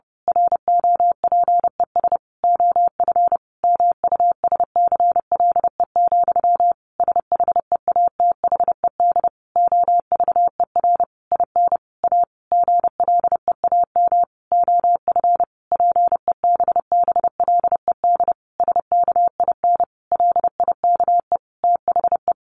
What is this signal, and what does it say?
Morse code, heard as a keyed tone, about 30 wpm, 700 Hz. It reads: ROPES OF MUSCLE, SHEATHED OVER IN A GLEAM OF PEBBLED SKIN LIKE THE
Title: A Sound Of Thunder 30wpm